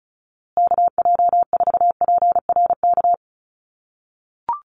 Morse code, heard as a keyed tone, about 34 wpm, 700 Hz. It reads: KJ4PRK
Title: USA Call Signs Repeated 3x 35wpm